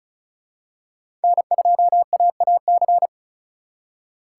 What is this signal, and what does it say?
Morse code, heard as a keyed tone, about 36 wpm, 700 Hz. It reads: N2AAC
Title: USA Call Signs - 4x Character Spacing - 35wpm